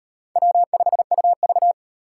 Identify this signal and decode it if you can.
Morse code, heard as a keyed tone, about 35 wpm, 700 Hz. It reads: W5UV